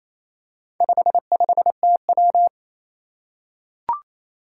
Morse code, heard as a keyed tone, about 27 wpm, 700 Hz. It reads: 55TW
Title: Four Challenging Letter-Number Combinations 28wpm